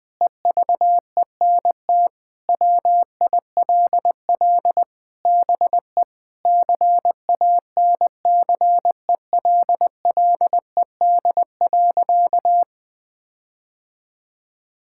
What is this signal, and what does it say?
Morse code, heard as a keyed tone, about 20 wpm, 700 Hz. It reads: EVENT WILL BE CANCELLED.
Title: Sentences with 8-Word Phrases - 20wpm